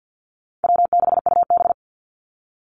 Morse code, heard as a keyed tone, about 48 wpm, 700 Hz. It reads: W6UB
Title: NA SOTA Activator Callsigns 50wpm